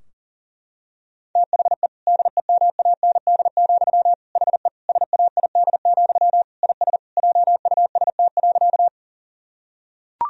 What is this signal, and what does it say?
Morse code, heard as a keyed tone, about 40 wpm, 700 Hz. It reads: THE DEMAND, HE SAID, IS JUST.
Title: Sentences from Top 1000 Words - Review Part 4 - 40wpm